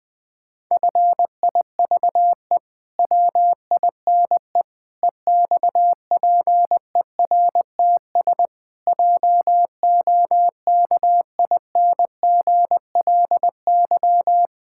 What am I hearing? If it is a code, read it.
Morse code, heard as a keyed tone, about 20 wpm, 700 Hz. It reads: FIVE WINE EXPERTS JOKINGLY
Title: Pangram Challenge v2 - 20wpm